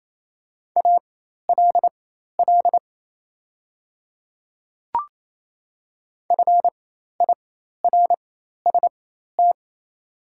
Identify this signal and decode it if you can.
Morse code, heard as a keyed tone, about 27 wpm, spaced as for 13 wpm, 690 Hz. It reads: ALL FIRST
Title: 4x Character Spacing - Top 300 Words - 28wpm